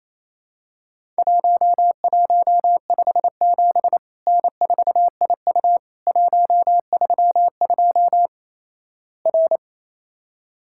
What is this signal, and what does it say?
Morse code, heard as a keyed tone, about 28 wpm, 700 Hz. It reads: 1157 N4IU 132 R
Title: Worked All Europe DX Contest - QTC Exchange - 28wpm